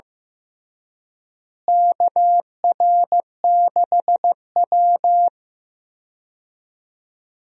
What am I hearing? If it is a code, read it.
Morse code, heard as a keyed tone, about 16 wpm, 700 Hz. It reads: KR6W
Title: USA Call Signs - 4 Characters - Repeated 2x - 15wpm